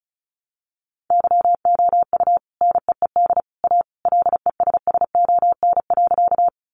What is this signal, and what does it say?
Morse code, heard as a keyed tone, about 34 wpm, 700 Hz. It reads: YOU NEED A LESSON.
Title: Sentences from Top 1200 Words 35wpm